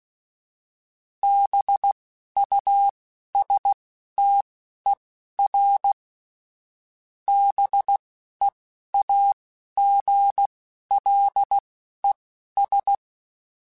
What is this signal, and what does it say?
Morse code, heard as a keyed tone, about 16 wpm, spaced as for 12 wpm, 800 Hz. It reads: BUSTER BEAGLES